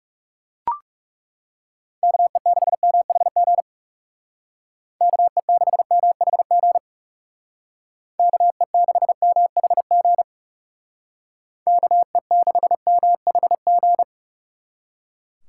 Morse code, heard as a keyed tone, about 37 wpm, 700 Hz. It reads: KE6MHG KE6MHG KE6MHG KE6MHG
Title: Speed Racing - USA 2x3 Call Signs 30wpm